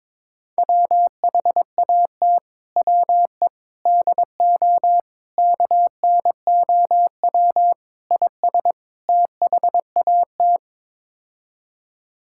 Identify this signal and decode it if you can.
Morse code, heard as a keyed tone, about 21 wpm, 700 Hz. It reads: WHAT WE DO KNOW IS THAT
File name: practice / 22wpm / Sets of 6 Words - 22wpm